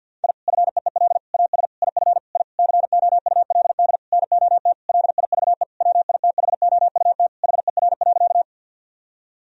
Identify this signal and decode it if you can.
Morse code, heard as a keyed tone, about 50 wpm, 700 Hz. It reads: I FEEL AS IF I COULD NOT LIVE WITHOUT HER.
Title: Sentences from Top 1000 Words - Review Part 3 - 50wpm